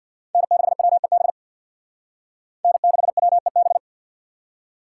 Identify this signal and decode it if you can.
Morse code, heard as a keyed tone, about 61 wpm, 700 Hz. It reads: N6WEB N6WEB
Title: Speed Racing - USA Call Signs 50wpm